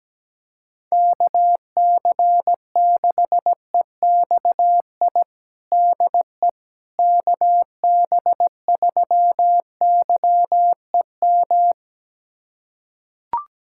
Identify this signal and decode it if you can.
Morse code, heard as a keyed tone, about 17 wpm, 700 Hz. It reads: KC6EXI DE KB3YEM
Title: USA Callsign DE Callsign 17wpm